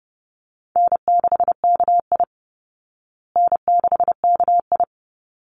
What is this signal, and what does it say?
Morse code, heard as a keyed tone, about 30 wpm, 700 Hz. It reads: N6KI N6KI
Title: Call Signs Repeated 3x - 30wpm